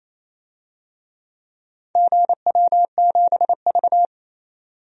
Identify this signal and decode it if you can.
Morse code, heard as a keyed tone, about 28 wpm, 700 Hz. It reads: GW7V
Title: DXCC-100 Call Signs - 4 Characters - 28wpm